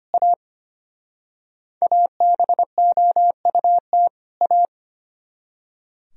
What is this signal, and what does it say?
Morse code, heard as a keyed tone, about 32 wpm, 700 Hz. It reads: A ABOUT A